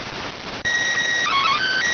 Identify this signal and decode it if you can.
SSTV (slow-scan television), PD 290.